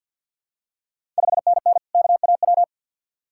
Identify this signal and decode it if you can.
Morse code, heard as a keyed tone, about 53 wpm, 700 Hz. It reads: 5NN KAW
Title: Ontario QSO Party Exchange 50wpm